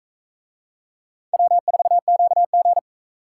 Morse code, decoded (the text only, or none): W4QG